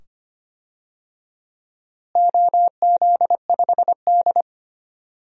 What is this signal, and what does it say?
Morse code, heard as a keyed tone, about 25 wpm, 700 Hz. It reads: OZ5D
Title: DXCC-100 Call Signs - 4 Characters - Repeated 2x - 25wpm